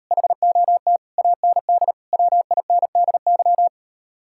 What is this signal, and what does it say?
Morse code, heard as a keyed tone, about 38 wpm, 700 Hz. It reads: HOT AND WINDY